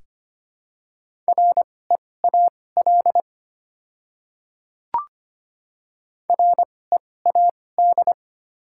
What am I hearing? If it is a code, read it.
Morse code, heard as a keyed tone, about 25 wpm, spaced as for 18 wpm, 700 Hz. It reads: REAL READ
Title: 2x Character Spacing - Top 300 Words - 25wpm